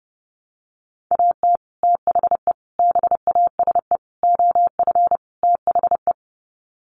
Morse code, heard as a keyed tone, about 30 wpm, 700 Hz. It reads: AT THE BASE OF THE
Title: Sets of 5 Words 30wpm